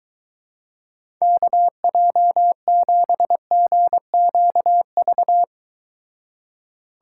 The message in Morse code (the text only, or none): KJ7GQV